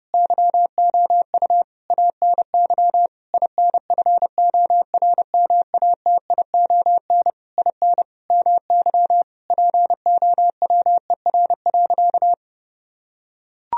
Morse code, heard as a keyed tone, about 30 wpm, 700 Hz. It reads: YOU ANY INFORMATION IN MY POWER.
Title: Sentences from Top 1000 Words - Review Part 3 - 30wpm